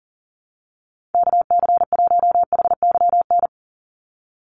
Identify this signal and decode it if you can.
Morse code, heard as a keyed tone, about 40 wpm, 700 Hz. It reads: KC1HYN
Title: USA Call Signs 40wpm